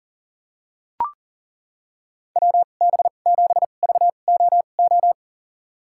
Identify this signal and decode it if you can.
Morse code, heard as a keyed tone, about 40 wpm, spaced as for 29 wpm, 700 Hz. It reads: WB7VOO
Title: USA Call Signs - 2x Character Spacing - 40wpm